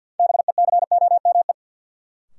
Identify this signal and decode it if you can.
Morse code, heard as a keyed tone, about 50 wpm, 700 Hz. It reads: BECOME
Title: Sets of 5 Words Encore - New Word Drill - 50wpm